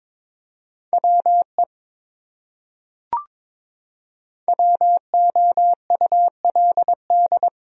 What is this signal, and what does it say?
Morse code, heard as a keyed tone, about 22 wpm, 700 Hz. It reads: WE WOULD
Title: Top 100 Words Repeated 22wpm